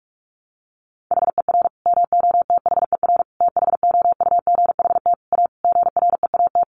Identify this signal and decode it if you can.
Morse code, heard as a keyed tone, about 44 wpm, 700 Hz. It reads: HER MOTHER THOUGHT A GREAT
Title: Sentences from Top 800 Words 45wpm